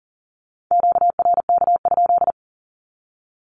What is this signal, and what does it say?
Morse code, heard as a keyed tone, about 40 wpm, 700 Hz. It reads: QRK?
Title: Q-Codes Repeated 3x 40wpm